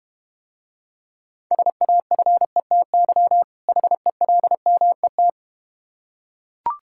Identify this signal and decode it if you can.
Morse code, heard as a keyed tone, about 31 wpm, 700 Hz. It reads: SAFETY HELMET
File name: Difficult Words - Speed-Racing with Hints - 25wpm